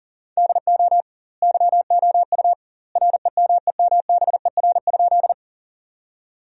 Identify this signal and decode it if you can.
Morse code, heard as a keyed tone, about 40 wpm, 700 Hz. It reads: DO YOU REMEMBER?
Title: 2x Word-Spacing - Sentences from Top 400 Words - 40wpm